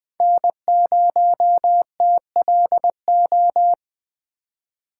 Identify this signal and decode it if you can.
Morse code, heard as a keyed tone, about 20 wpm, 700 Hz. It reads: N0TLO
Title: USA Call Signs - 2x Character Spacing - 20wpm